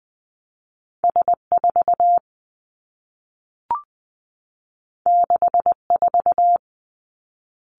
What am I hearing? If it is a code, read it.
Morse code, heard as a keyed tone, about 20 wpm, 700 Hz. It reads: S 4 6 4
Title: Two Challenging Letter-Number Combinations 20wpm